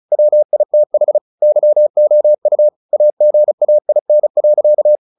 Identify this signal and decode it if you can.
Morse code, heard as a keyed tone, about 35 wpm, 600 Hz. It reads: WITH YOU AGAIN.
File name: the old man and the sea / the old man and the sea 1 35wpm